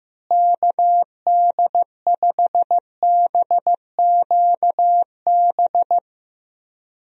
Morse code, heard as a keyed tone, about 15 wpm, 700 Hz. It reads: KD5BQB
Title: USA Call Signs - 6 Characters - Repeated 2x - 15wpm